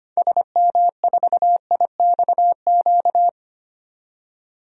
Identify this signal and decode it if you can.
Morse code, heard as a keyed tone, about 25 wpm, 700 Hz. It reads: SM4IXQ
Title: DXCC-100 Call Signs - 6 Characters - 25wpm